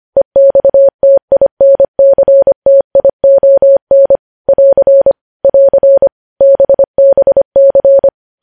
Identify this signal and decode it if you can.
Morse code, heard as a keyed tone, about 25 wpm, 550 Hz. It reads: EXTINCTION <AR> <AR> BBC
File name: podcast / News headlines-1x-25x25